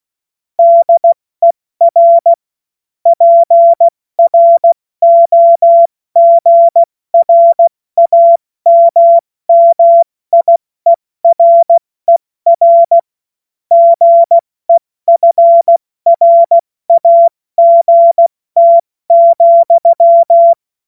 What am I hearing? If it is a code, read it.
Morse code, heard as a keyed tone, about 16 wpm, 680 Hz. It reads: DER PROGRAMMIERER GEFRAGT,
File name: Morsekurs / Witze / Witz 05. April 2022 70 F80